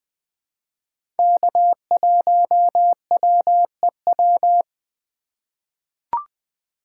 Morse code, heard as a keyed tone, about 20 wpm, 700 Hz. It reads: K1WEW
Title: USA Call Signs - 4x Character Spacing - 20wpm